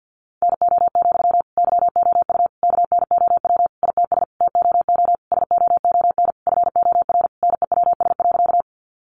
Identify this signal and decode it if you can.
Morse code, heard as a keyed tone, about 50 wpm, 700 Hz. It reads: NO, YOU KNOW ITS TOO SOON FOR NEWS.
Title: Sentences from Top 800 Words 50wpm